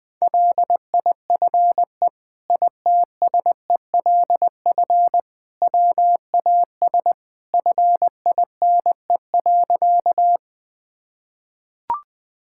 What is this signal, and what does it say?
Morse code, heard as a keyed tone, about 20 wpm, 700 Hz. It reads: LIFE ITSELF WAS FINE.
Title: Sentences from Top 1000 Words - Review Part 5 - 20wpm